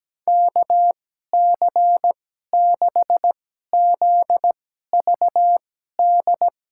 Morse code, heard as a keyed tone, about 17 wpm, spaced as for 12 wpm, 700 Hz. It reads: KC6ZVD
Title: USA Call Signs - 6 Characters - 2x Character Spacing - 17wpm